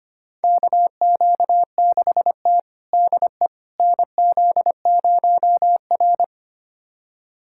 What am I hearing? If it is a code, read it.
Morse code, heard as a keyed tone, about 25 wpm, 700 Hz. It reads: KQ6T DE NZ0R
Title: USA Callsign DE Callsign - 4 Characters - 25wpm